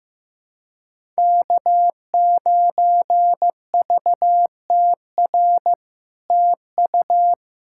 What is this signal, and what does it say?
Morse code, heard as a keyed tone, about 15 wpm, 700 Hz. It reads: K9VTR TU
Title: SOTA QSO Practice 15wpm